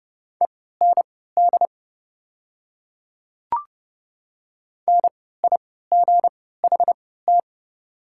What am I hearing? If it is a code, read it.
Morse code, heard as a keyed tone, about 30 wpm, spaced as for 17 wpm, 700 Hz. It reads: END NIGHT
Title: 3x Character Spacing - Top 300 Words - 30wpm